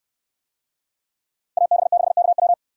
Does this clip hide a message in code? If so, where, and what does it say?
Yes: Morse code from 1.6 to 2.5 s, heard as a keyed tone, about 64 wpm, 700 Hz. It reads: AB6KW